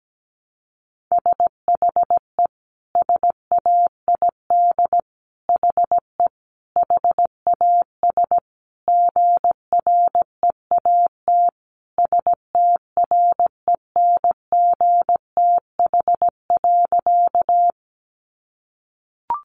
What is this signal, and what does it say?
Morse code, heard as a keyed tone, about 17 wpm, 700 Hz. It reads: SHE SAID HE HAS GREAT STRENGTH.
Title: Sentences from Top 1300 Words 17wpm